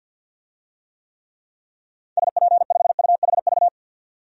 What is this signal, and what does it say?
Morse code, heard as a keyed tone, about 47 wpm, 700 Hz. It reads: SP5UHV
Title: DXCC-100 Call Signs - 6 Characters - 50wpm